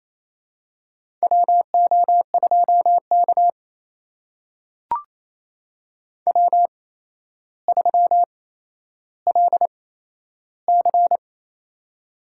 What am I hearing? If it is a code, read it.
Morse code, heard as a keyed tone, about 28 wpm, spaced as for 12 wpm, 700 Hz. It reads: WO2K W3LC
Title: USA Call Signs - 4 Characters - 8x Character Spacing - 28wpm